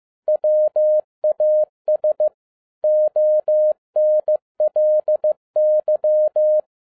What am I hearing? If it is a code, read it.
Morse code, heard as a keyed tone, about 15 wpm, 610 Hz. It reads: WAS ONLY
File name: 2011-10-mp3 / n12-JL3YMV 15wpm